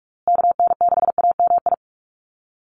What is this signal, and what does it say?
Morse code, heard as a keyed tone, about 44 wpm, 700 Hz. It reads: KN6AMI
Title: USA Call Signs Repeated 3x 45wpm